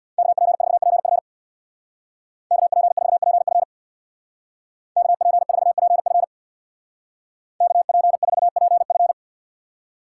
Morse code, heard as a keyed tone, about 63 wpm, 700 Hz. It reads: KP4PF KP4PF KP4PF KP4PF